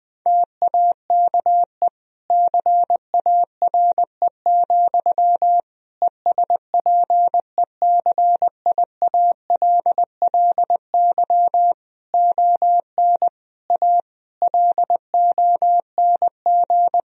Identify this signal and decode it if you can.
Morse code, heard as a keyed tone, about 20 wpm, 700 Hz. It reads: TAKE CARE, ESPECIALLY ON A LONG
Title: Sentences from Top 1000 Words - Review Part 4 - 20wpm